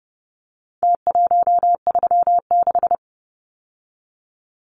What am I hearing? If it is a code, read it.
Morse code, heard as a keyed tone, about 30 wpm, 700 Hz. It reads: T136
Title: Numbers - Contest Style - 4-Digit 30wpm